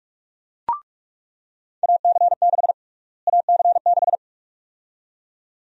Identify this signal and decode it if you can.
Morse code, heard as a keyed tone, about 41 wpm, 700 Hz. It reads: AC6 AC6
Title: USA Call Sign Prefixes - Repeated 2x - 45wpm